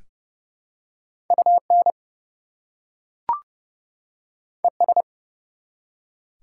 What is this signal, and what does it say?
Morse code, heard as a keyed tone, about 29 wpm, 700 Hz. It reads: UN ES
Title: Prefixes and Suffixes 30wpm